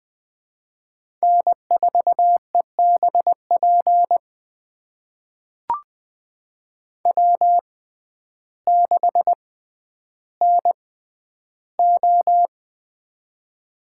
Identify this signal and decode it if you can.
Morse code, heard as a keyed tone, about 20 wpm, spaced as for 11 wpm, 700 Hz. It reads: N4EBP W6NO
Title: USA Call Signs - 6x Character Spacing - 20wpm